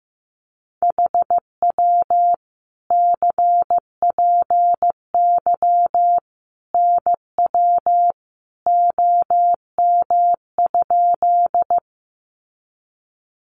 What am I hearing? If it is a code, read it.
Morse code, heard as a keyed tone, about 15 wpm, 700 Hz. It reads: HW CPY NW OM?
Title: Common QSO Phrases 15wpm